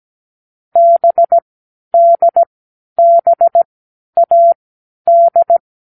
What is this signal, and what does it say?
Morse code, heard as a keyed tone, about 17 wpm, spaced as for 10 wpm, 680 Hz. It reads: BDBAD
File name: k7qo / Lesson6(A-D)